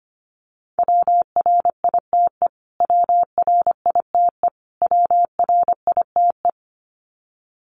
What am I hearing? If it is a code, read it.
Morse code, heard as a keyed tone, about 25 wpm, 700 Hz. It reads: WRITE WRITE WRITE
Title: Top 500 Words Repeated 25wpm